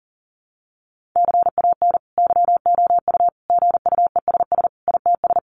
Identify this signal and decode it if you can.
Morse code, heard as a keyed tone, about 40 wpm, 700 Hz. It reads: CAN YOU GUESS ITS
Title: Sentences from Top 1500 Words 40wpm